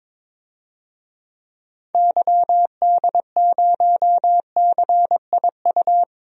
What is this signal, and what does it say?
Morse code, heard as a keyed tone, about 21 wpm, 700 Hz. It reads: YD0CIU
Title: DXCC-100 Call Signs - 6 Characters - 22wpm